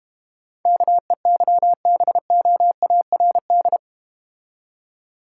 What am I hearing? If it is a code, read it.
Morse code, heard as a keyed tone, about 31 wpm, 700 Hz. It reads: KEYBOARD
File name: Easy Words - Speed-Racing with Hints - 25wpm